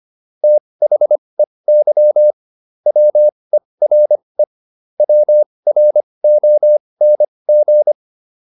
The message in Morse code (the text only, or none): THEY WERE WRONG